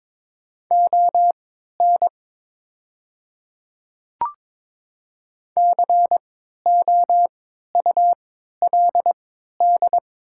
Morse code, heard as a keyed tone, about 22 wpm, spaced as for 12 wpm, 690 Hz. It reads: ON COULD
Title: 3x Character Spacing - Top 100 Words - 22wpm